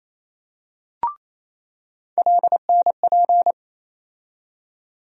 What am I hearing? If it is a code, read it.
Morse code, heard as a keyed tone, about 28 wpm, 700 Hz. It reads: LNP
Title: USA Call Sign Postfixes - 3 Letters 28wpm